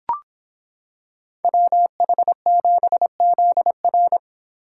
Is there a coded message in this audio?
Yes: Morse code from 1.4 to 4.2 s, heard as a keyed tone, about 26 wpm, 700 Hz. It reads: WH7ZR